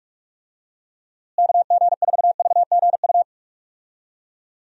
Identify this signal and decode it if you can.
Morse code, heard as a keyed tone, about 42 wpm, 700 Hz. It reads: KG4VGU
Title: USA Call Signs - 6 Characters - Repeated 2x - 45wpm